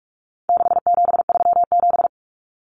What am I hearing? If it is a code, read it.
Morse code, heard as a keyed tone, about 44 wpm, 700 Hz. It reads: 6737
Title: Common Cut Numbers - 4-Digit 45wpm